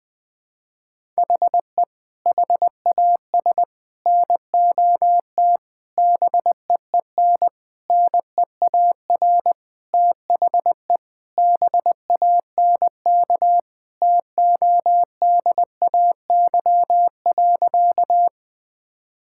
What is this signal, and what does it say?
Morse code, heard as a keyed tone, about 20 wpm, 700 Hz. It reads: HE HAS NOT BEEN NEAR THE BANK TODAY.
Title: Sentences from Top 1000 Words - Review Part 6 - 20wpm